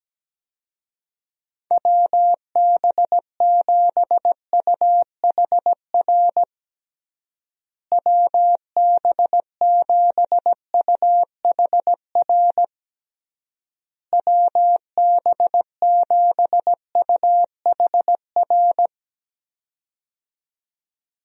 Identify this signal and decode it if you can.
Morse code, heard as a keyed tone, about 17 wpm, 700 Hz. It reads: WB7UHR WB7UHR WB7UHR
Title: USA Call Signs - 6 Characters - Repeated 3x - 17wpm